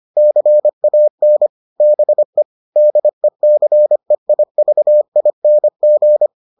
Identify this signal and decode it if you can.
Morse code, heard as a keyed tone, about 25 wpm, 600 Hz. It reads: CAN BE DECEIVING